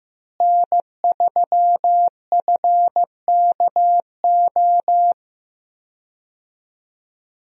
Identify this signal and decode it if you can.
Morse code, heard as a keyed tone, about 15 wpm, 700 Hz. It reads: N3FKO
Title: USA Call Signs - 5 Characters - Repeated 2x - 15wpm